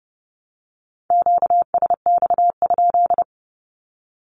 Morse code, heard as a keyed tone, about 30 wpm, 700 Hz. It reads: QSX?
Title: Q-Codes 30wpm